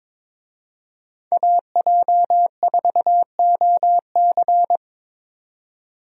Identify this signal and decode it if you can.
Morse code, heard as a keyed tone, about 22 wpm, 700 Hz. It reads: AJ4OC
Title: USA Call Signs - 5 Characters - 22wpm